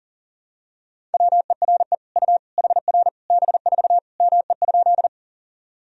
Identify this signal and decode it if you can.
Morse code, heard as a keyed tone, about 37 wpm, 700 Hz. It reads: WERE U HR B4 ME?
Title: Sentences from Top 200 Words and Abbreviations 40wpm